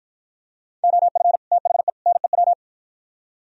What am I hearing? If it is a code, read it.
Morse code, heard as a keyed tone, about 52 wpm, 700 Hz. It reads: OF THE NEW